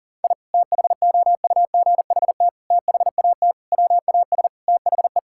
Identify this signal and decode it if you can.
Morse code, heard as a keyed tone, about 37 wpm, 700 Hz. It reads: I THOUGHT THAT WAS THE
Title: Sentences from Top 1000 Words - Review Part 5 - 40wpm